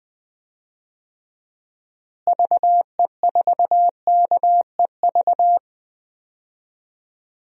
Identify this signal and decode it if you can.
Morse code, heard as a keyed tone, about 21 wpm, 700 Hz. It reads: VE4KEV
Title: DXCC-100 Call Signs - 6 Characters - Repeated 3x - 20wpm